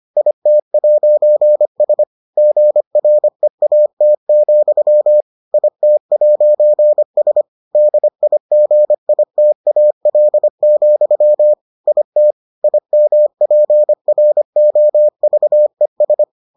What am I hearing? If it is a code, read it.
Morse code, heard as a keyed tone, about 25 wpm, 600 Hz. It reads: IT'S GREAT, IT'S DIGITAL, IT IMPROVES